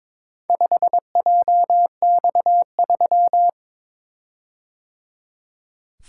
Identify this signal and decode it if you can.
Morse code, heard as a keyed tone, about 22 wpm, 700 Hz. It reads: 5JX3
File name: practice / 22wpm / Four Letter-Number Combinations - 22wpm